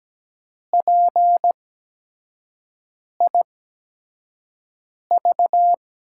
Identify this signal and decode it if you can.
Morse code, heard as a keyed tone, about 17 wpm, spaced as for 5 wpm, 700 Hz. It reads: PIV
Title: USA Call Signs - 8x Character Spacing - 17wpm